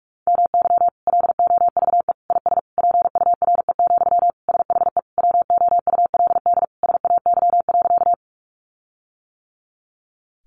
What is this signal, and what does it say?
Morse code, heard as a keyed tone, about 44 wpm, 700 Hz. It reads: MY LOVE IS PURE, SHE WOULD SAY.
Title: Sentences from Top 1500 Words 45wpm